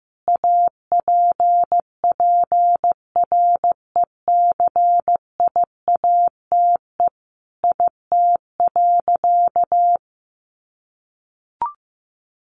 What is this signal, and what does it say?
Morse code, heard as a keyed tone, about 15 wpm, 700 Hz. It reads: APPRECIATE IT.
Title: Sentences from Top 1300 Words 15wpm